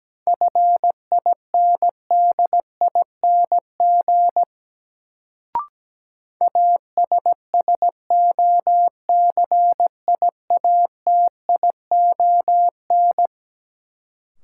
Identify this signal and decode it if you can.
Morse code, heard as a keyed tone, about 17 wpm, 700 Hz. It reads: FINDING ASSOCIATION
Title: Sentences from Top 1100 Words - New Word Drill - 17wpm